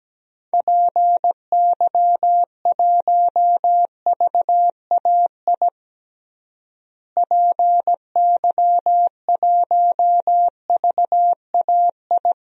Morse code, heard as a keyed tone, about 17 wpm, 700 Hz. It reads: PY1VAI PY1VAI
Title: DXCC-100 Call Signs - 6 Characters - Repeated 2x - 17wpm